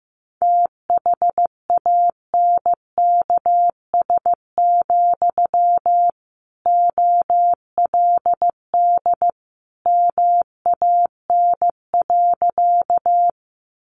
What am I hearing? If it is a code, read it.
Morse code, heard as a keyed tone, about 15 wpm, 700 Hz. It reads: THANKS, OLD MAN.
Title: Sentences from Top 600 Words 15wpm